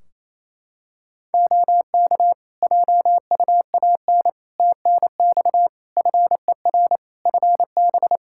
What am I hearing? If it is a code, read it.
Morse code, heard as a keyed tone, about 28 wpm, 700 Hz. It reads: OK JUAN TNX FER FB
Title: Practice QSO - Exchange 4 - Ending - 28wpm